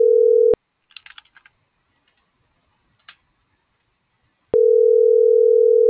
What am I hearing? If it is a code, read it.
Ringback tone (North America), 440 + 480 Hz.